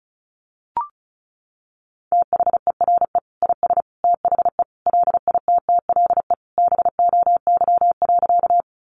Morse code, heard as a keyed tone, about 34 wpm, 700 Hz. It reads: THERE IS THE LITTLE BOY.
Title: Sentences from Top 200 Words 35wpm